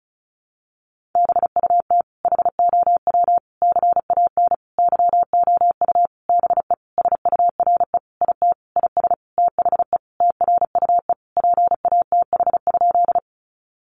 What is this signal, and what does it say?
Morse code, heard as a keyed tone, about 34 wpm, 700 Hz. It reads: BUT HOW CAN YOU BE SURE IT IS THE TRUE PATH?
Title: Sentences from Top 1400 Words 35wpm